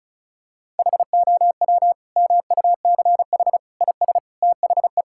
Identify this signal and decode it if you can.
Morse code, heard as a keyed tone, about 36 wpm, 700 Hz. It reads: HOW MUCH IS THE
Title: Sentences from Top 1000 Words - Review Part 5 - 35wpm